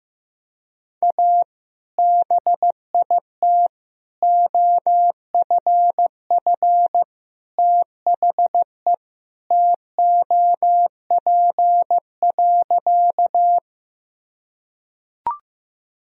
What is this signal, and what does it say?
Morse code, heard as a keyed tone, about 15 wpm, 700 Hz. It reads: A BIT OFF THE TOP.
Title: Sentences from Top 1000 Words - Review Part 2 - 15wpm